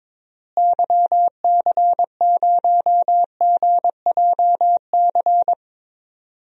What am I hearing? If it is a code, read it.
Morse code, heard as a keyed tone, about 22 wpm, 700 Hz. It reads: YC0GJC
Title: DXCC-100 Call Signs - 6 Characters - Repeated 3x - 22wpm